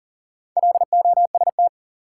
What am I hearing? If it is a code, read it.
Morse code, heard as a keyed tone, about 40 wpm, 700 Hz. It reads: LOST
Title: Sentences from Top 1400 Words - New Word Drill Alt - 40wpm